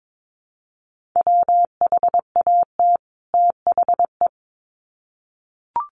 Morse code, heard as a keyed tone, about 22 wpm, 700 Hz. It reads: WHAT THE
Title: Sets of 2 Words 22wpm